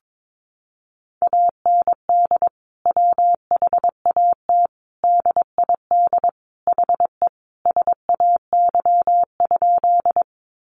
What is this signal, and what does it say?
Morse code, heard as a keyed tone, about 22 wpm, 700 Hz. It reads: AND WHAT DID HE SAY?
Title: Sentences from Top 200 Words 22wpm